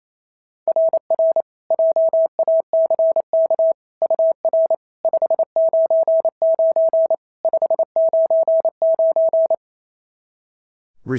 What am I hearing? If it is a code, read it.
Morse code, heard as a keyed tone, about 25 wpm, 650 Hz. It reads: RR JACK UR 599 599
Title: SOTA QSO Practice 28wpm